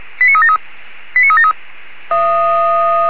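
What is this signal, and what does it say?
Five-tone selective call (ZVEI), heard as quick runs of five tones, tones of about 70 ms. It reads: 89373 89373
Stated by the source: ZVEI five-tone selective call received by SDR, code 85371.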